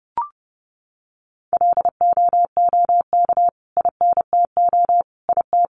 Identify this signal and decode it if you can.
Morse code, heard as a keyed tone, about 30 wpm, 700 Hz. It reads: LOOK INTO IT
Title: Sentences from Top 1400 Words 30wpm